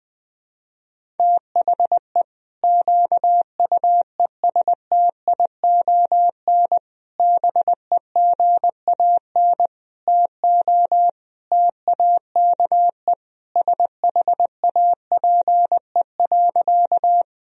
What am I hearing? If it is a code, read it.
Morse code, heard as a keyed tone, about 20 wpm, 700 Hz. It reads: THE QUESTION BEGAN TO TAKE SHAPE.
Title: Sentences from Top 1000 Words - Review Part 3 - 20wpm